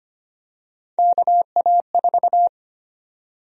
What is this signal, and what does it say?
Morse code, heard as a keyed tone, about 25 wpm, 700 Hz. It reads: KA4
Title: USA Call Sign Prefixes - 25wpm